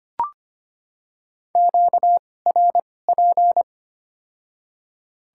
Morse code, heard as a keyed tone, about 25 wpm, spaced as for 18 wpm, 700 Hz. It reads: QRP
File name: Top 90 QSO Elements - 2x Character Spacing - 25wpm